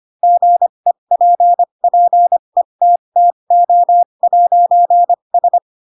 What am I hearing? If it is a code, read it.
Morse code, heard as a keyed tone, about 25 wpm, spaced as for 22 wpm, 700 Hz. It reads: GEPPETTO'S
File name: Pinocchio / Pinocchio ch2 18 25 ik1hge